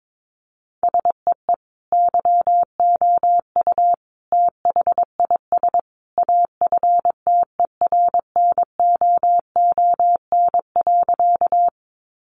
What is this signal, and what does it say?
Morse code, heard as a keyed tone, about 22 wpm, 700 Hz. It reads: SEE YOU THIS AFTERNOON.
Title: Sentences from Top 1500 Words 22wpm